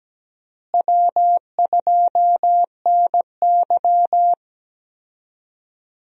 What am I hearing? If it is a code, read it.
Morse code, heard as a keyed tone, about 17 wpm, 700 Hz. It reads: W2NY
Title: USA Call Signs - 4 Characters - Repeated 3x - 17wpm